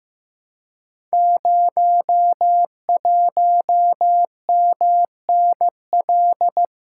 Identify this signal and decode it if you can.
Morse code, heard as a keyed tone, about 15 wpm, 700 Hz. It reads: 01MNL
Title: Code Groups - 5 Letter:Numbers - 15wpm